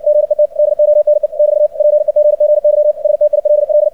Morse code, heard as a keyed tone, about 60 wpm, 600 Hz. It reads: GET DONE <BT> PEGGY STILL
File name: archived / CRSnet